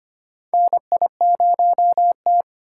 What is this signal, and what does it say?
Morse code, heard as a keyed tone, about 25 wpm, 700 Hz. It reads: NI0T
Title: USA Call Signs - 4 Characters - 2x Character Spacing - 25wpm